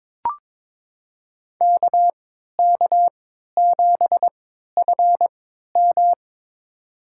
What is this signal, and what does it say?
Morse code, heard as a keyed tone, about 21 wpm, spaced as for 12 wpm, 700 Hz. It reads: KK7FM
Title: USA Call Signs - 5 Characters - 3x Character Spacing - 22wpm